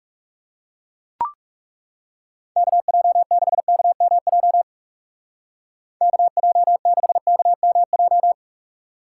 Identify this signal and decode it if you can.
Morse code, heard as a keyed tone, about 37 wpm, 700 Hz. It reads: KJ6KMJ KJ6KMJ
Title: Speed Racing - USA 2x3 Call Signs 30wpm